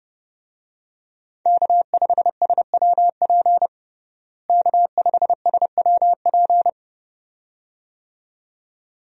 Morse code, heard as a keyed tone, about 31 wpm, 700 Hz. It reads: K5SWP K5SWP
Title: USA Call Signs - 5 Characters - Repeated 2x - 30wpm